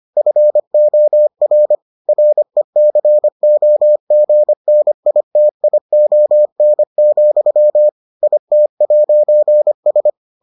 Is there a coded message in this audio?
Yes: Morse code from 0.2 to 10.1 s, heard as a keyed tone, about 25 wpm, 600 Hz. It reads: FOR RECOGNITION, IT'S